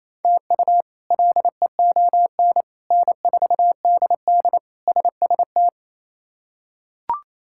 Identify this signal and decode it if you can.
Morse code, heard as a keyed tone, about 27 wpm, 700 Hz. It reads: TU LEON N4DD SST
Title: K1USN SST QSO Practice - 28wpm